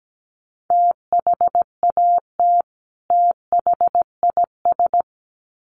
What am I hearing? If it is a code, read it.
Morse code, heard as a keyed tone, about 17 wpm, 700 Hz. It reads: THAT THIS
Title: Sets of 2 Words 17wpm